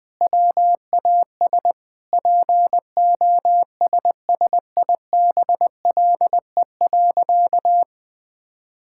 Morse code, heard as a keyed tone, about 20 wpm, 700 Hz. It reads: WAS POSSIBLE.
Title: Sentences with 5-Word Phrases - 20wpm